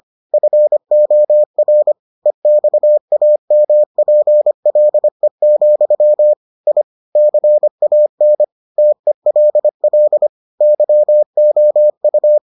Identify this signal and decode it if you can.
Morse code, heard as a keyed tone, about 25 wpm, 600 Hz. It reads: FOR EXAMPLE, I CAN TELL YOU